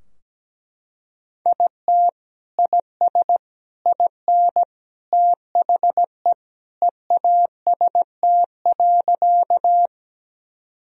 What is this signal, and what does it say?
Morse code, heard as a keyed tone, about 17 wpm, 700 Hz. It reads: IT IS IN THE EAST.
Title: Sentences from Top 1000 Words - Review Part 3 - 17wpm